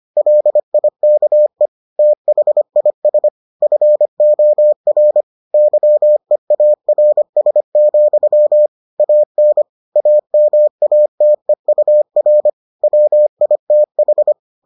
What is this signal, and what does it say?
Morse code, heard as a keyed tone, about 25 wpm, 600 Hz. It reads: LIKE THIS FOR YEARS, AN AMATEUR WITH